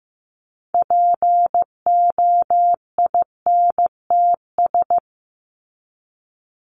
Morse code, heard as a keyed tone, about 15 wpm, 700 Hz. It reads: POINTS
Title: ES:S Suffix - Top 100 Words with Base Words 15wpm